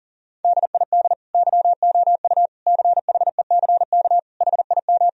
Morse code, heard as a keyed tone, about 40 wpm, 700 Hz. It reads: DID YOU CHECK HIM
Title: Sentences from Top 1000 Words - Review Part 3 - 40wpm